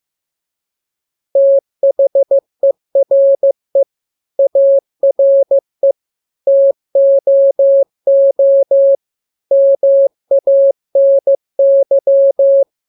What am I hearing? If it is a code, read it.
Morse code, heard as a keyed tone, about 15 wpm, 550 Hz. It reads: THERE ARE TOO MANY
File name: jokes / joke 20 15 0000